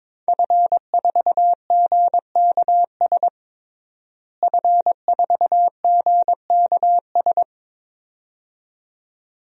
Morse code, heard as a keyed tone, about 22 wpm, 700 Hz. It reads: F4GKS F4GKS
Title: DXCC-100 Call Signs - 5 Characters - Repeated 2x - 22wpm